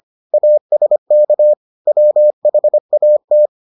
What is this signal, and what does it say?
Morse code, heard as a keyed tone, about 25 wpm, 600 Hz. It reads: ASK WHAT